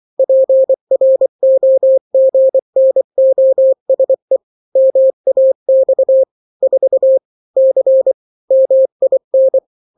Morse code, heard as a keyed tone, about 24 wpm, 530 Hz. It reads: PROGNOSE MAX 4 C MIN
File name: daily / 23wpm